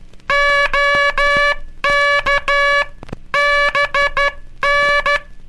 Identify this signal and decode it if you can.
Morse code, heard as a keyed tone, about 12 wpm, 1700 Hz. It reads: OK BN
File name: cursos / Curso CW mp3 / CW Aula - 18